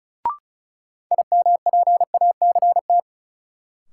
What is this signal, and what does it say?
Morse code, heard as a keyed tone, about 36 wpm, 700 Hz. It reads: IMPACT
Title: Sentences from Top 1000 Words - Review Part 6 - New Word Drill - 35wpm